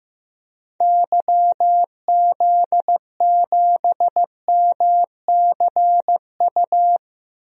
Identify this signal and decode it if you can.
Morse code, heard as a keyed tone, about 15 wpm, 700 Hz. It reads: YZ7MCU
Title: DXCC-100 Call Signs - 6 Characters - 15wpm